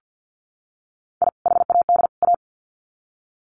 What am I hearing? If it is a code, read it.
Morse code, heard as a keyed tone, about 50 wpm, 700 Hz. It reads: I HAD A
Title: Sets of 3 Words 50wpm